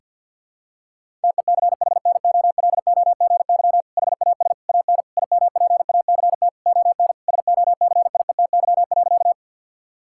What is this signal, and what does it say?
Morse code, heard as a keyed tone, about 49 wpm, 700 Hz. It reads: TECHNOLOGY HAS AN IMPACT ON SOCIETY.
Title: Sentences from Top 1000 Words - Review Part 6 - 50wpm